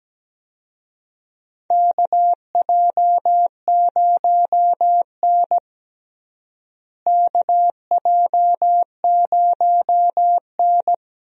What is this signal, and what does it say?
Morse code, heard as a keyed tone, about 17 wpm, 700 Hz. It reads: KJ0N KJ0N
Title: USA Call Signs - 4 Characters - Repeated 3x - 17wpm